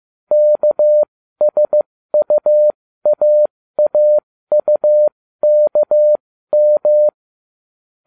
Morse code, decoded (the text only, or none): KSUAAUKM